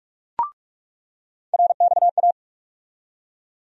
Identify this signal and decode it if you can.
Morse code, heard as a keyed tone, about 47 wpm, 700 Hz. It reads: RXA
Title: USA Call Sign Postfixes - 3 Letters 45wpm